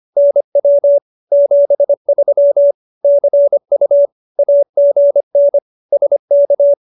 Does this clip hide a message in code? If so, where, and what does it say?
Yes: Morse code from 0.2 to 6.7 s, heard as a keyed tone, about 25 wpm, 570 Hz. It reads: NW 73 CU AGN SK